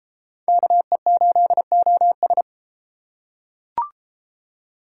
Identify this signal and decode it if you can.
Morse code, heard as a keyed tone, about 32 wpm, 700 Hz. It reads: KE8OS